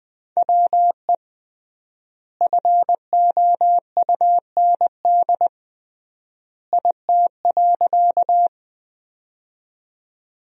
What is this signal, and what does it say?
Morse code, heard as a keyed tone, about 20 wpm, 700 Hz. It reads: WE FOUND IT.
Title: 3x Word-Spacing - Sentences from Top 300 Words - 20wpm